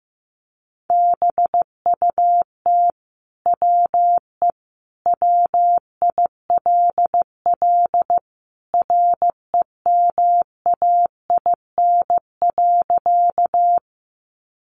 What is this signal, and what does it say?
Morse code, heard as a keyed tone, about 15 wpm, 700 Hz. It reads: BUT WE WILL REMAIN.
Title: Sentences from Top 1200 Words 15wpm